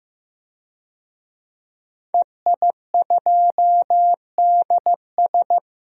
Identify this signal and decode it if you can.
Morse code, heard as a keyed tone, about 15 wpm, 700 Hz. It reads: EI2DS
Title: DXCC-100 Call Signs - 5 Characters - Repeated 3x - 15wpm